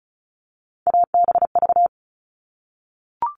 Morse code, heard as a keyed tone, about 34 wpm, 700 Hz. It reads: ABV
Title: All Cut Numbers - 3-Digit 35wpm